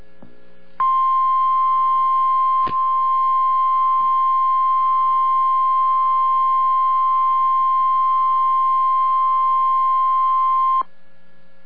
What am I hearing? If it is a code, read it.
Emergency Alert System (SAME) tones (the 1050 Hz attention tone). It is heard from 0.8 to 10.9 s.